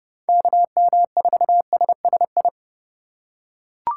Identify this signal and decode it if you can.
Morse code, heard as a keyed tone, about 31 wpm, 700 Hz. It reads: KM4SSI
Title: USA Call Signs - 6 Characters - 4x Character Spacing - 30wpm